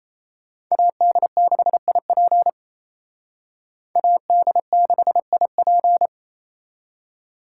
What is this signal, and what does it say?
Morse code, heard as a keyed tone, about 37 wpm, 700 Hz. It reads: AD6IP AD6IP